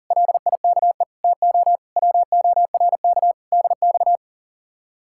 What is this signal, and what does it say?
Morse code, heard as a keyed tone, about 48 wpm, 700 Hz. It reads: LIKE TO WORK DX
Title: Speed-Racing Common QSO Phrases - 40wpm